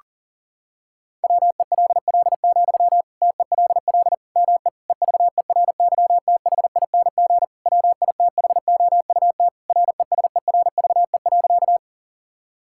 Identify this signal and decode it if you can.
Morse code, heard as a keyed tone, about 40 wpm, 700 Hz. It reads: WELL, TELL ME EVERYTHING WITHOUT RESERVE.
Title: Sentences from Top 1000 Words - Review Part 6 - 40wpm